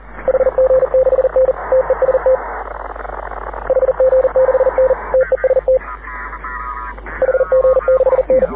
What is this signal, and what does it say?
Morse code, heard as a keyed tone, about 40 wpm, 520 Hz. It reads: HG6N TEST HG6N TEST HG6N
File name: AUDIO2019 / HG6N CW SSB MIX